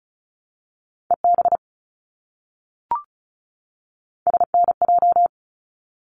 Morse code, heard as a keyed tone, about 36 wpm, 700 Hz. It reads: EB SNJ
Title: ARRL Sections 35wpm